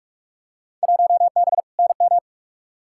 Morse code, heard as a keyed tone, about 46 wpm, 700 Hz. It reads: 1B NM
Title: Field Day Exchanges Encore - 45wpm